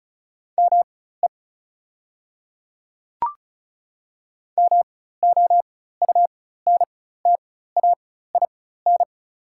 Morse code, heard as a keyed tone, about 36 wpm, spaced as for 16 wpm, 690 Hz. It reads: ME MOUNTAIN
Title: 4x Character Spacing - Top 500 Words - 35wpm